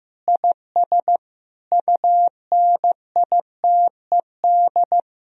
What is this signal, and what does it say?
Morse code, heard as a keyed tone, about 15 wpm, 700 Hz. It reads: IS UNITED
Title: Sentences from Top 1000 Words - Review Part 4 - 15wpm